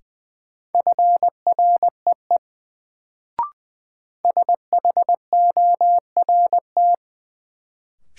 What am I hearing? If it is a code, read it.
Morse code, heard as a keyed tone, about 20 wpm, 700 Hz. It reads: FREE SHORT
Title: Sets of 5 Words Encore - New Word Drill - 20wpm